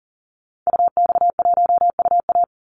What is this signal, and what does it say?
Morse code, heard as a keyed tone, about 40 wpm, 700 Hz. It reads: UX1UA
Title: Call Signs - 40wpm